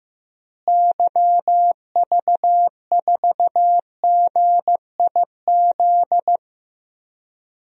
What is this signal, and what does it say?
Morse code, heard as a keyed tone, about 15 wpm, 700 Hz. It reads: YV4GIZ
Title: DXCC-100 Call Signs - 6 Characters - Repeated 2x - 15wpm